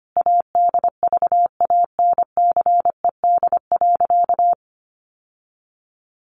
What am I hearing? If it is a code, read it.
Morse code, heard as a keyed tone, about 25 wpm, 700 Hz. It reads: ADVANCED.
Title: Sentences from Top 1200 Words 25wpm